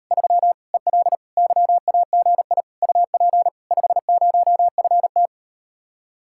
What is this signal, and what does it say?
Morse code, heard as a keyed tone, about 38 wpm, 700 Hz. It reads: 3 EL YAGI UP 50FT